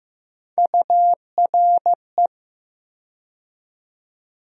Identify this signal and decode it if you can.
Morse code, heard as a keyed tone, about 15 wpm, 700 Hz. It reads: URE
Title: USA Call Sign Postfixes - 3 Letters 15wpm